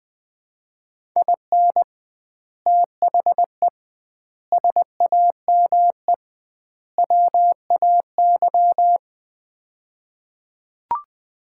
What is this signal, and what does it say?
Morse code, heard as a keyed tone, about 21 wpm, 700 Hz. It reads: IN THE SAME WAY
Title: 2x Word-Spacing - Sets of 4 Words - 20wpm